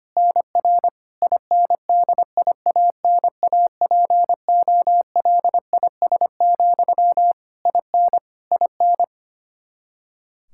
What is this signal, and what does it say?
Morse code, heard as a keyed tone, about 25 wpm, 700 Hz. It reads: NR INDIANAPOLIS, IN IN
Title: Practice QSO - Exchange 1 - Beginning - 25wpm